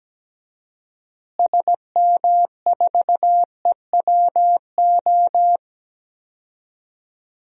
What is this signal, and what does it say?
Morse code, heard as a keyed tone, about 17 wpm, 700 Hz. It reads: SM4EWO
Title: DXCC-100 Call Signs - 6 Characters - Repeated 3x - 17wpm